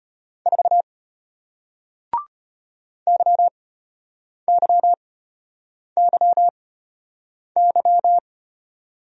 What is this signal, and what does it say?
Morse code, heard as a keyed tone, about 34 wpm, spaced as for 8 wpm, 700 Hz. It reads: 4 YYYY